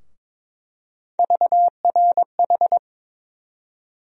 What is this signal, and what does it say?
Morse code, heard as a keyed tone, about 22 wpm, 700 Hz. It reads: VRH
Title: USA Call Sign Postfixes - 3 Letters 22wpm